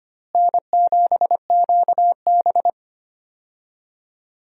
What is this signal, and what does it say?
Morse code, heard as a keyed tone, about 25 wpm, 700 Hz. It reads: N7QB
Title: USA Call Signs - 4 Characters - Repeated 3x - 25wpm